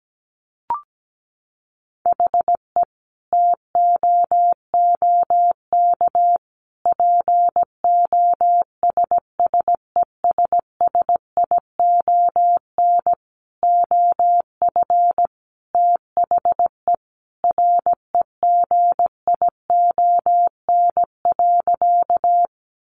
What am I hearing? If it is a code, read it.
Morse code, heard as a keyed tone, about 17 wpm, 700 Hz. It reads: HE TOOK POSSESSION OF THE REGION.
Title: Sentences from Top 1500 Words 17wpm